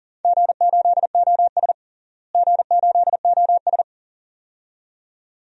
Morse code, heard as a keyed tone, about 40 wpm, 700 Hz. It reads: G8OS G8OS
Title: DXCC-100 Call Signs - 4 Characters - Repeated 2x - 40wpm